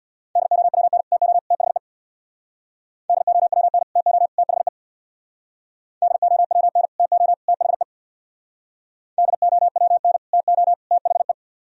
Morse code, heard as a keyed tone, about 64 wpm, 700 Hz. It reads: DOWN TO THE DOWN TO THE DOWN TO THE DOWN TO THE